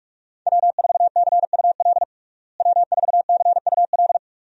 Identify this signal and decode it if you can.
Morse code, heard as a keyed tone, about 46 wpm, 700 Hz. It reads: W4CUL W4CUL
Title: USA Call Signs - 5 Characters - Repeated 2x - 45wpm